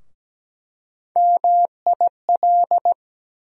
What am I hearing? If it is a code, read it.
Morse code, heard as a keyed tone, about 17 wpm, 700 Hz. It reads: MIL
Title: USA Call Sign Postfixes - 3 Letters 17wpm